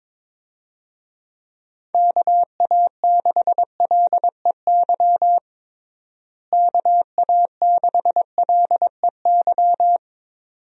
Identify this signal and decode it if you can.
Morse code, heard as a keyed tone, about 22 wpm, 700 Hz. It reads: KA6LEY KA6LEY
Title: USA Call Signs - 6 Characters - Repeated 2x - 22wpm